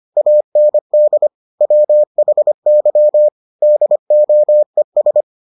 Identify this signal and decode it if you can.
Morse code, heard as a keyed tone, about 25 wpm, 600 Hz. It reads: AND WHY DOES